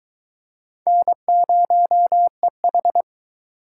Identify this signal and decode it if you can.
Morse code, heard as a keyed tone, about 20 wpm, 700 Hz. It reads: N0EH